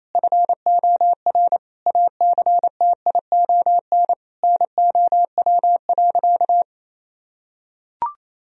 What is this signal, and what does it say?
Morse code, heard as a keyed tone, about 27 wpm, 700 Hz. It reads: FOR ACTION NOW.
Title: Sentences from Top 1000 Words - Review Part 4 - 28wpm